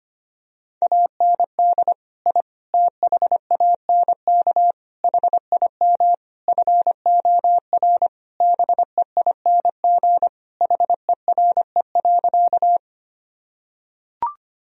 Morse code, heard as a keyed tone, about 25 wpm, 700 Hz. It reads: AND I THANK HIM FOR BEING HERE.
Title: Sentences from Top 1000 Words - Review Part 3 - 25wpm